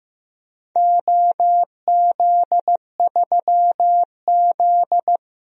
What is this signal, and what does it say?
Morse code, heard as a keyed tone, about 15 wpm, 700 Hz. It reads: OZ3Z
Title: DXCC-100 Call Signs - 4 Characters - 15wpm